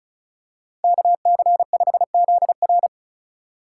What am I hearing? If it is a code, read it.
Morse code, heard as a keyed tone, about 34 wpm, 700 Hz. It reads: KC5ZR
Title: USA Call Signs - 5 Characters - 4x Character Spacing - 35wpm